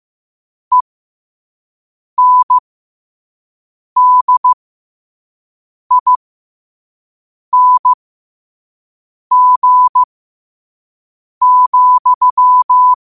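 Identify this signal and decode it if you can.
Morse code, heard as a keyed tone, about 15 wpm, spaced as for 5 wpm, 1000 Hz. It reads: ENDING,
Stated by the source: 5 wpm Morse.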